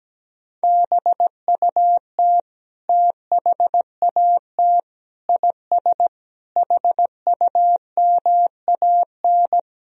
Morse code, heard as a keyed tone, about 17 wpm, 700 Hz. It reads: BUT THAT IS HUMAN
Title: Sentences from Top 1000 Words - Review Part 3 - 17wpm